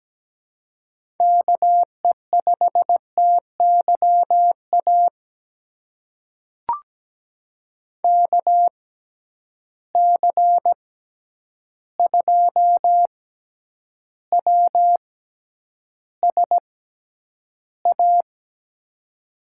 Morse code, heard as a keyed tone, about 17 wpm, spaced as for 9 wpm, 700 Hz. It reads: KE5TYA KC2WSA
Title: USA Call Signs - 6x Character Spacing - 17wpm